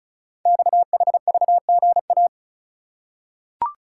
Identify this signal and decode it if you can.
Morse code, heard as a keyed tone, about 36 wpm, 690 Hz. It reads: XHVGA
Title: Code Groups - 5 Letters - 35wpm